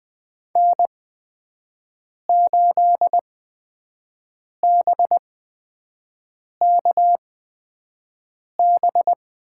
Morse code, heard as a keyed tone, about 20 wpm, spaced as for 5 wpm, 700 Hz. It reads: N8BKB
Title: USA Call Signs - 8x Character Spacing - 20wpm